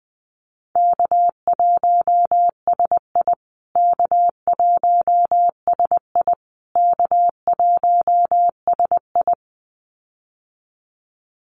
Morse code, heard as a keyed tone, about 20 wpm, 700 Hz. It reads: K1SI K1SI K1SI
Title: USA Call Signs Repeated 3x 20wpm